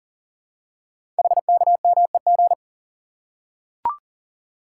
Morse code, heard as a keyed tone, about 39 wpm, 700 Hz. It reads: HKMEG
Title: Code Groups - 5 Letters - 40wpm